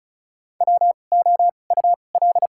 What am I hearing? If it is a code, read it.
Morse code, heard as a keyed tone, about 34 wpm, spaced as for 25 wpm, 700 Hz. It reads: WOUL
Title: 2x Character Spacing - Top 100 Words - 35wpm